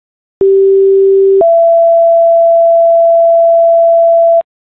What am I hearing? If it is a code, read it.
Two-tone page, heard as a one-second tone, then a three-second tone, about 390 then 670 Hz, with noise. It is heard from 0.4 to 4.4 s.